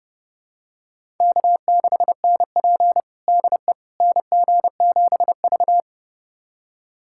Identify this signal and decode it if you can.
Morse code, heard as a keyed tone, about 32 wpm, 700 Hz. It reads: K6NP DE NG7V
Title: USA Callsign DE Callsign - 4 Characters - 30wpm